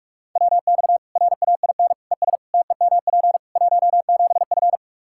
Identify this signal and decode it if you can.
Morse code, heard as a keyed tone, about 44 wpm, 700 Hz. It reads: WX RAIN ES TEMP 17F
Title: Practice QSO - Exchange 2 - Info - 45wpm